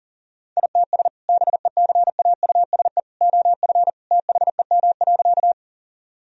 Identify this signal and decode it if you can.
Morse code, heard as a keyed tone, about 40 wpm, 700 Hz. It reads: ITS BECAUSE OF THEM.
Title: Sentences from Top 1000 Words - Review Part 3 - 40wpm